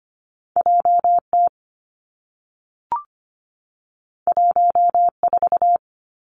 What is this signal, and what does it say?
Morse code, heard as a keyed tone, about 25 wpm, 700 Hz. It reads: J T 1 4
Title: Two Challenging Letter-Number Combinations 25wpm